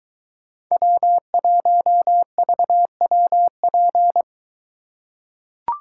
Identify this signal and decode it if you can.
Morse code, heard as a keyed tone, about 21 wpm, 700 Hz. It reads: W1VWP